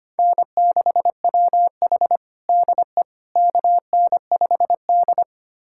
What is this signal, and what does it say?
Morse code, heard as a keyed tone, about 25 wpm, 700 Hz. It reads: N6WH DE KN5D
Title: USA Callsign DE Callsign - 4 Characters - 25wpm